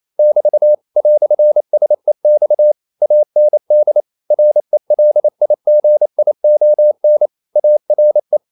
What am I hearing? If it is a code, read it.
Morse code, heard as a keyed tone, about 28 wpm, 600 Hz. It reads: <BT> "SEX AND RELIGION ARE
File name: CW / Practice / EnglishQuotations0128